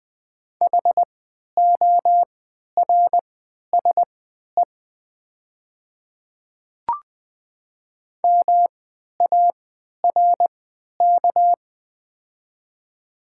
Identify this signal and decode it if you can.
Morse code, heard as a keyed tone, about 20 wpm, spaced as for 11 wpm, 700 Hz. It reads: HORSE MARK
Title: 3x Character Spacing - Top 300 Words - 20wpm